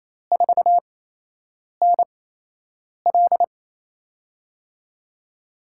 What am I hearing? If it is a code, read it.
Morse code, heard as a keyed tone, about 28 wpm, spaced as for 8 wpm, 700 Hz. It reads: 4NL
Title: USA Call Signs - 8x Character Spacing - 28wpm